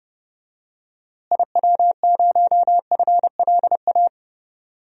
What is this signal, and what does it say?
Morse code, heard as a keyed tone, about 32 wpm, 700 Hz. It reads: IW0FLA
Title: DXCC-100 Call Signs - 6 Characters - Repeated 3x - 30wpm